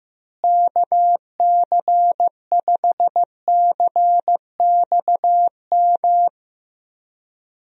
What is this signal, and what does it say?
Morse code, heard as a keyed tone, about 15 wpm, 700 Hz. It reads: KC5CXM
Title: USA Call Signs - 6 Characters - Repeated 3x - 15wpm